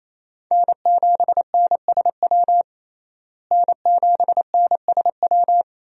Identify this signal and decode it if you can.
Morse code, heard as a keyed tone, about 28 wpm, 700 Hz. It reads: N7NSW N7NSW
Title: USA Call Signs - 5 Characters - Repeated 2x - 28wpm